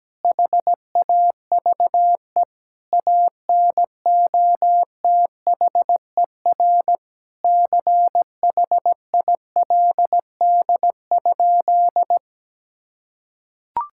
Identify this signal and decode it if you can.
Morse code, heard as a keyed tone, about 17 wpm, 700 Hz. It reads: HAVE ANOTHER CHILD?
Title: Sentences from Top 1600 Words 17wpm